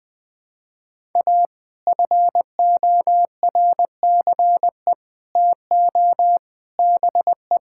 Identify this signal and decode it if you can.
Morse code, heard as a keyed tone, about 20 wpm, 700 Hz. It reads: A FORCE TO BE
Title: Sets of 6 Words - 20wpm